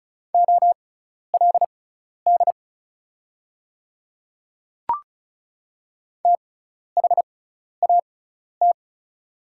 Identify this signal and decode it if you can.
Morse code, heard as a keyed tone, about 36 wpm, spaced as for 12 wpm, 690 Hz. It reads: OLD THAT
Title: 6x Character Spacing - Top 300 Words - 35wpm